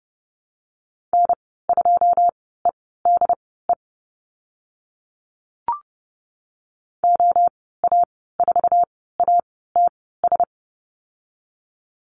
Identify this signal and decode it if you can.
Morse code, heard as a keyed tone, about 30 wpm, spaced as for 17 wpm, 700 Hz. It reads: N2EDE OA4ATS
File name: Lesson 15g - Number 5 - Call Signs 30wpm